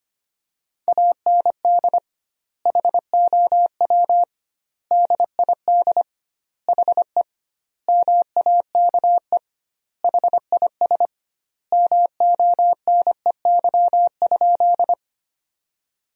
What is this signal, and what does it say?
Morse code, heard as a keyed tone, about 25 wpm, 700 Hz. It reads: AND HOW DID HE MAKE HIS MONEY?
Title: 2x Word-Spacing - Sentences from Top 400 Words - 25wpm